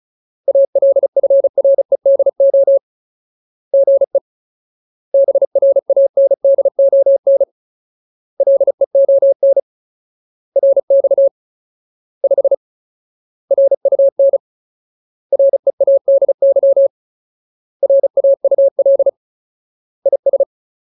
Morse code, heard as a keyed tone, about 34 wpm, 550 Hz. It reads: ALFREDO GE BRANDON LEON RX 5 RUN READY RAUL IS
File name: Ultimate Word List / 35WPM Ultimate